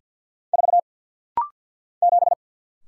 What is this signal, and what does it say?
Morse code, heard as a keyed tone, about 50 wpm, 700 Hz. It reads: 4 7
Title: Single Letter-Number - Rapid-Fire 50wpm